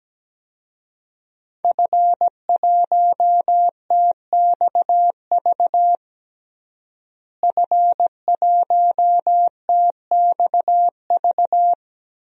Morse code, heard as a keyed tone, about 17 wpm, 700 Hz. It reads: F1TXV F1TXV
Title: DXCC-100 Call Signs - 5 Characters - Repeated 2x - 17wpm